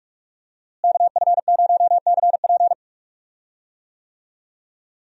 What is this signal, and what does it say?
Morse code, heard as a keyed tone, about 45 wpm, 700 Hz. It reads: KF0CP
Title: USA Call Signs - 5 Characters - Repeated 3x - 45wpm